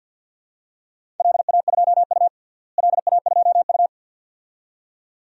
Morse code, heard as a keyed tone, about 48 wpm, 700 Hz. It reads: LA2U LA2U
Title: DXCC-100 Call Signs - 4 Characters - Repeated 2x - 50wpm